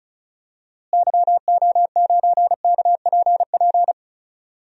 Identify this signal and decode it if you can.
Morse code, heard as a keyed tone, about 36 wpm, 700 Hz. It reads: YO9KPP
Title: DXCC-100 Call Signs - 6 Characters - Repeated 2x - 35wpm